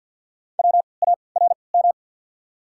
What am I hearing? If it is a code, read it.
Morse code, heard as a keyed tone, about 47 wpm, spaced as for 28 wpm, 690 Hz. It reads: WARM
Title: 3x Character Spacing - Top 500 Words - 50wpm